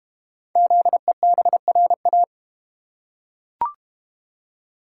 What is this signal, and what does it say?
Morse code, heard as a keyed tone, about 32 wpm, 700 Hz. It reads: ZEBRA